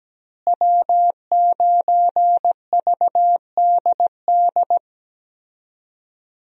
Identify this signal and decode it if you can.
Morse code, heard as a keyed tone, about 17 wpm, 700 Hz. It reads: W9VDD
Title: USA Call Signs - 5 Characters - Repeated 2x - 17wpm